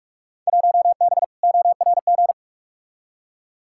Morse code, heard as a keyed tone, about 46 wpm, 700 Hz. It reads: 1B ORG
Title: Field Day Exchanges Encore - 45wpm